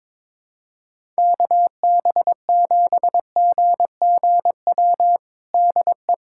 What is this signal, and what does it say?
Morse code, heard as a keyed tone, about 22 wpm, 700 Hz. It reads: KB7GGW DE
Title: USA Callsign DE Callsign 22wpm